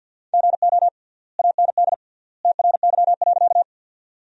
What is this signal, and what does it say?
Morse code, heard as a keyed tone, about 50 wpm, 700 Hz. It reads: GO AND TRY.
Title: 3x Word-Spacing - Sentences from Top 200 Words - 50wpm